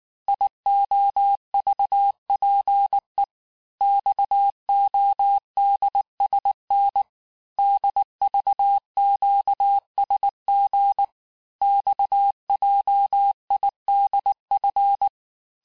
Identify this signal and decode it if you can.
Morse code, heard as a keyed tone, about 19 wpm, 800 Hz. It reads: IOVPE XODSN DVQSG XJIDF
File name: cours-du-mercredi / audios / cours-869-texte-CHIFFRE-vitesse-18-MPM